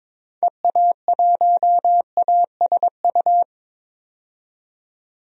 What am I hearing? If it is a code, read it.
Morse code, heard as a keyed tone, about 22 wpm, 700 Hz. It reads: EA1ASU
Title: DXCC-100 Call Signs - 6 Characters - 22wpm